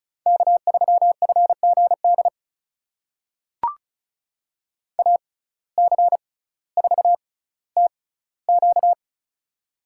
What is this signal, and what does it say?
Morse code, heard as a keyed tone, about 34 wpm, spaced as for 18 wpm, 700 Hz. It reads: K3FGD AC4TQ
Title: USA Call Signs - 5 Characters - 6x Character Spacing - 35wpm